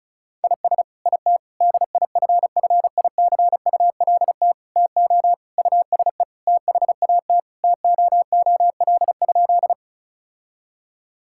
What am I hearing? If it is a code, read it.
Morse code, heard as a keyed tone, about 36 wpm, 700 Hz. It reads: IS IT DIFFICULT TO USE THAT TOOL?
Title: Sentences from Top 1000 Words - Review Part 6 - 35wpm